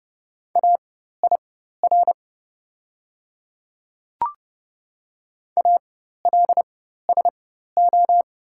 Morse code, heard as a keyed tone, about 31 wpm, spaced as for 14 wpm, 700 Hz. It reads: AIR ALSO
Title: 4x Character Spacing - Top 200 Words - 30wpm